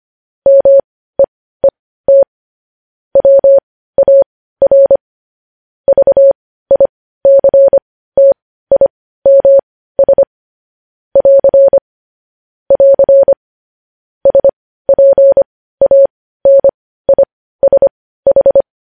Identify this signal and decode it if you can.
Morse code, heard as a keyed tone, about 25 wpm, spaced as for 15 wpm, 550 Hz. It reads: MEET WAR VICTIMS <AR> <AR> SPANISH
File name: podcast / News headlines-1x-25x15